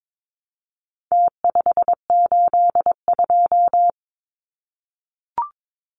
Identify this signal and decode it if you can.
Morse code, heard as a keyed tone, about 22 wpm, 700 Hz. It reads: T582
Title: Numbers - Contest Style - 4-Digit 22wpm